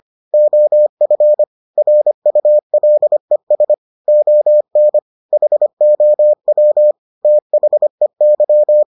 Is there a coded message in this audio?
Yes: Morse code from 0.3 to 8.8 s, heard as a keyed tone, about 25 wpm, 600 Hz. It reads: OF RULES ON HOW THEY